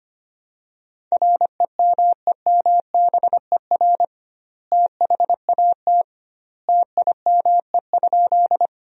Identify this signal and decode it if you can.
Morse code, heard as a keyed tone, about 25 wpm, 700 Hz. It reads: REMEMBER THAT TIME?
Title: 2x Word-Spacing - Sentences from Top 400 Words - 25wpm